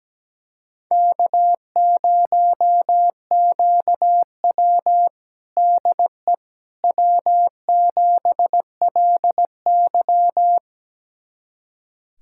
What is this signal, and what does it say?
Morse code, heard as a keyed tone, about 17 wpm, 700 Hz. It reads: K0QW DE W7LY
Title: USA Callsign DE Callsign - 4 Characters - 17wpm